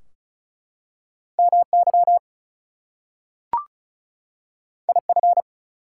Morse code, heard as a keyed tone, about 36 wpm, 700 Hz. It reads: MY IF
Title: Two Letter Words - 35wpm